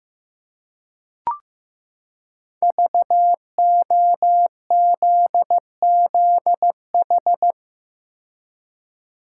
Morse code, heard as a keyed tone, about 15 wpm, 700 Hz. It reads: VOZZH
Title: Code Groups - 5 Letter:Numbers - 15wpm